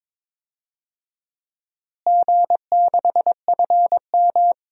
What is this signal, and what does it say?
Morse code, heard as a keyed tone, about 22 wpm, 700 Hz. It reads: G6FM
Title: DXCC-100 Call Signs - 4 Characters - Repeated 3x - 22wpm